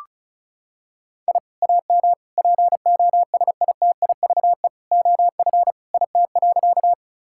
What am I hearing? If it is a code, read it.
Morse code, heard as a keyed tone, about 34 wpm, 700 Hz. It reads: I AM POSITIVE OF IT.
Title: Sentences from Top 1000 Words - Review Part 5 - 35wpm